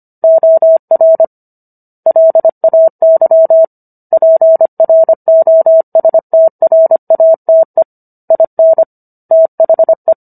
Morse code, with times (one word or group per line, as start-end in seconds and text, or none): 0.2-1.2 OR
2.1-3.6 LAY
4.1-7.8 PROSTRATE
8.3-8.8 IN
9.3-10.1 THE